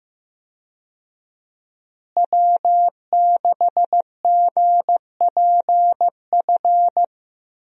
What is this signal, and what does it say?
Morse code, heard as a keyed tone, about 15 wpm, 700 Hz. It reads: W6GPF
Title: USA Call Signs - 5 Characters - Repeated 3x - 15wpm